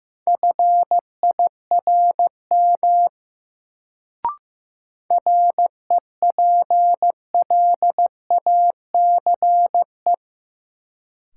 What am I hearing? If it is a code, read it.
Morse code, heard as a keyed tone, about 16 wpm, 700 Hz. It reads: FIRM REPLACE
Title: Sentences from Top 1400 Words - New Word Drill Alt - 15wpm